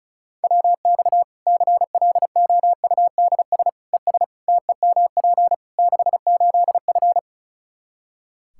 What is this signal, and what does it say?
Morse code, heard as a keyed tone, about 36 wpm, 700 Hz. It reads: WX CLOUDS ES TEMP 68F
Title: Practice QSO - Exchange 2 - Info - 35wpm